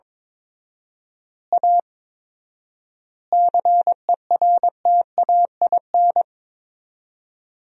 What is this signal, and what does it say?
Morse code, heard as a keyed tone, about 22 wpm, 700 Hz. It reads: A CERTAIN
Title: 4x Word-Spacing - Sentences from Top 500 Words - 22wpm